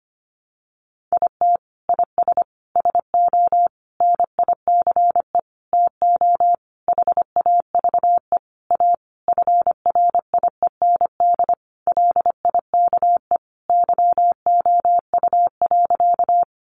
Morse code, heard as a keyed tone, about 25 wpm, 700 Hz. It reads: IT IS SO NICE TO HAVE A FRIEND LIKE YOU.
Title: Sentences from Top 1100 Words 25wpm